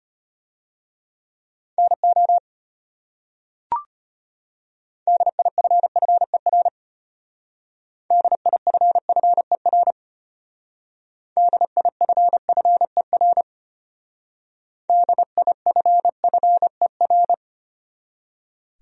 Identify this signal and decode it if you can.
Morse code, heard as a keyed tone, about 33 wpm, 700 Hz. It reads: NO DIFFER DIFFER DIFFER DIFFER